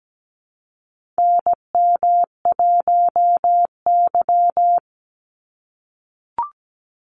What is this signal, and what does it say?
Morse code, heard as a keyed tone, about 17 wpm, 700 Hz. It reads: NM1Y
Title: Call Signs - 17wpm